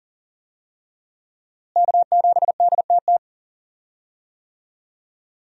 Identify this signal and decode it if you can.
Morse code, heard as a keyed tone, about 38 wpm, 700 Hz. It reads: K7DTT
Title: USA Call Signs - 5 Characters - Repeated 3x - 40wpm